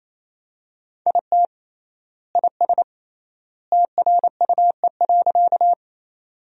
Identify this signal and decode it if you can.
Morse code, heard as a keyed tone, about 28 wpm, 700 Hz. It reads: IT IS TRUE.
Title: 3x Word-Spacing - Sentences from Top 400 Words - 28wpm